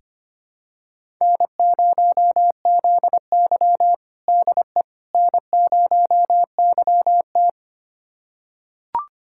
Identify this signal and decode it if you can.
Morse code, heard as a keyed tone, about 25 wpm, 700 Hz. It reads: N0ZY DE N0YT
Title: USA Callsign DE Callsign - 4 Characters - 25wpm